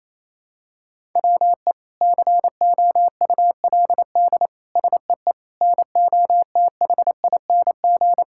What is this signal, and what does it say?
Morse code, heard as a keyed tone, about 28 wpm, 700 Hz. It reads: WE COULD SEE NOTHING
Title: Sentences from Top 1000 Words - Review Part 5 - 28wpm